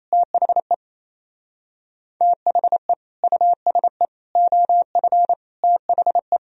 Morse code, heard as a keyed tone, about 33 wpm, 700 Hz. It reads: THE THE USE OF THE
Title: Speed-Racing - Sets of 4 Words 28wpm